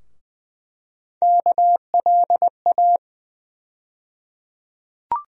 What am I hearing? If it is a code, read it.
Morse code, heard as a keyed tone, about 20 wpm, 700 Hz. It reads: KLA
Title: USA Call Sign Postfixes - 3 Letters 20wpm